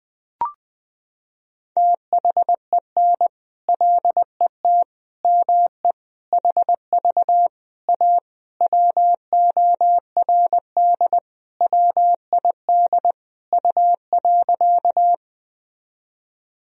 Morse code, heard as a keyed tone, about 21 wpm, 700 Hz. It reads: THEN LET ME HV A WORD WID U.
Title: Sentences from Top 300 Words and Abbreviations 20wpm